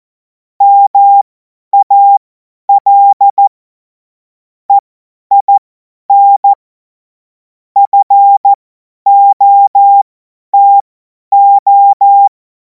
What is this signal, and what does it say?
Morse code, heard as a keyed tone, about 14 wpm, spaced as for 10 wpm, 800 Hz. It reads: MAL EIN FOTO
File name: 50 F70